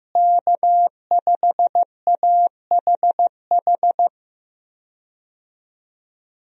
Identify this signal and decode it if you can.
Morse code, heard as a keyed tone, about 16 wpm, 700 Hz. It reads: K5AHH
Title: USA Call Signs - 5 Characters - Repeated 3x - 15wpm